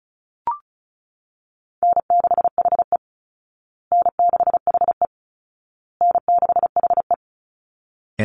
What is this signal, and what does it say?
Morse code, heard as a keyed tone, about 34 wpm, 700 Hz. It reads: N6HE N6HE N6HE
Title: Easy Call Signs Repeated 3x - 35wpm